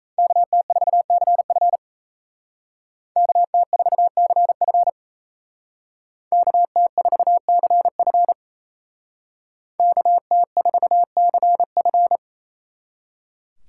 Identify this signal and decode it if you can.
Morse code, heard as a keyed tone, about 36 wpm, 700 Hz. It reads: KT4CF KT4CF KT4CF KT4CF